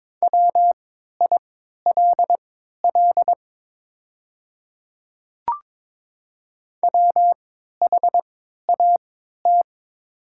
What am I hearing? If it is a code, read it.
Morse code, heard as a keyed tone, about 22 wpm, spaced as for 12 wpm, 700 Hz. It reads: WILL WHAT
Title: 3x Character Spacing - Top 100 Words - 22wpm